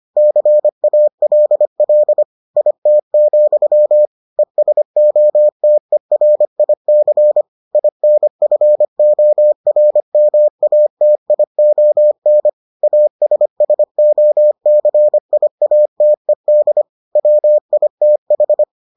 Morse code, heard as a keyed tone, about 25 wpm, 600 Hz. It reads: CALL IT, ESOTERIC INFORMATION ASSOCIATED WITH